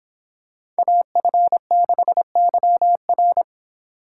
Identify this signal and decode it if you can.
Morse code, heard as a keyed tone, about 24 wpm, 700 Hz. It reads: AF6YR